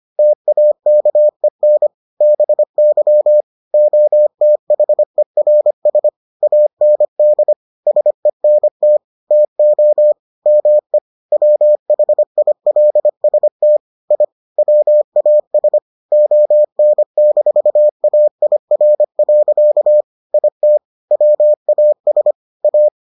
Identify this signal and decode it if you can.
Morse code, heard as a keyed tone, about 25 wpm, 600 Hz. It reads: TAKEN BY OTHERS AND SENT TO ME WHILST I WAS ON-AIR. IT WAS A